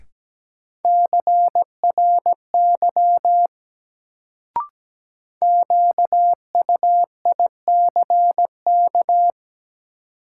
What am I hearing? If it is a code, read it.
Morse code, heard as a keyed tone, about 17 wpm, 700 Hz. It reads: CRY QUICK
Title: Sentences from Top 500 Words - New Word Drill - 17wpm